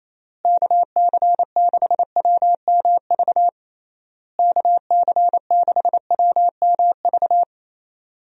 Morse code, heard as a keyed tone, about 28 wpm, 700 Hz. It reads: KC6WMV KC6WMV
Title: USA Call Signs - 6 Characters - Repeated 3x - 28wpm